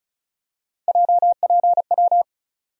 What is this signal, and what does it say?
Morse code, heard as a keyed tone, about 34 wpm, 700 Hz. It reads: JPW
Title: USA Call Sign Postfixes - 3 Letters 35wpm